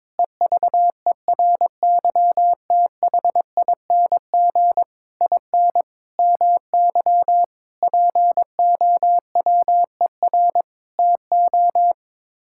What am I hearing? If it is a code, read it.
Morse code, heard as a keyed tone, about 22 wpm, 700 Hz. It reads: EVERYTHING IN MY POWER TO
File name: practice / 22wpm / Sets of 8 Words - 22wpm